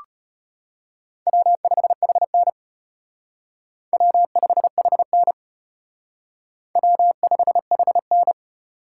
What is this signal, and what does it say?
Morse code, heard as a keyed tone, about 33 wpm, 700 Hz. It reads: W5HN W5HN W5HN